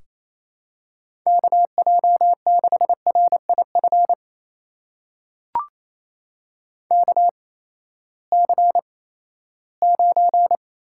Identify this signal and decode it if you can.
Morse code, heard as a keyed tone, about 28 wpm, spaced as for 16 wpm, 700 Hz. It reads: KJ6RIF KC9
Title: USA Call Signs - 6 Characters - 8x Character Spacing - 28wpm